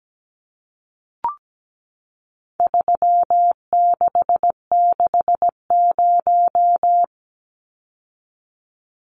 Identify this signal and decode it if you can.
Morse code, heard as a keyed tone, about 17 wpm, 700 Hz. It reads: 3660
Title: Numbers - 4-Digit 17wpm